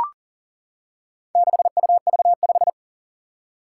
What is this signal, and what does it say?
Morse code, heard as a keyed tone, about 40 wpm, 700 Hz. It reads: 6UV5
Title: Four Challenging Letter-Number Combinations 40wpm